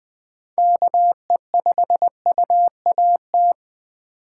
Morse code, heard as a keyed tone, about 20 wpm, 700 Hz. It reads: KE5UAT